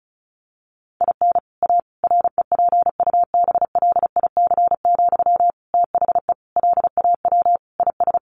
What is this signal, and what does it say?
Morse code, heard as a keyed tone, about 34 wpm, 700 Hz. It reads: IN A REPUBLIC, THE LAW IS
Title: Sentences from Top 1500 Words 35wpm